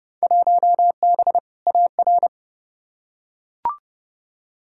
Morse code, heard as a keyed tone, about 30 wpm, 700 Hz. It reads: 1B AR
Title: Field Day Exchanges Encore - 30wpm